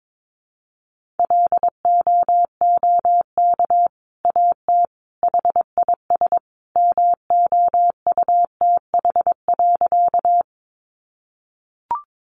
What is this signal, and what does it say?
Morse code, heard as a keyed tone, about 22 wpm, 700 Hz. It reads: LOOK AT HIS MOUTH.
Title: Sentences from Top 1500 Words 22wpm